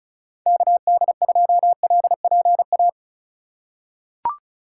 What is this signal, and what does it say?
Morse code, heard as a keyed tone, about 36 wpm, 700 Hz. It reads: KD2LPA
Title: USA Call Signs - 6 Characters - 3x Character Spacing - 35wpm